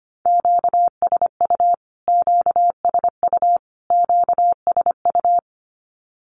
Morse code, heard as a keyed tone, about 25 wpm, 700 Hz. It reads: QSU QSU QSU
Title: Q-Codes Repeated 3x 25wpm